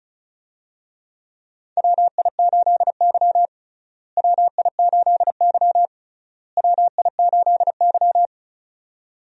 Morse code, heard as a keyed tone, about 36 wpm, 700 Hz. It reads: WI8Y WI8Y WI8Y
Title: USA Call Signs - 4 Characters - Repeated 3x - 35wpm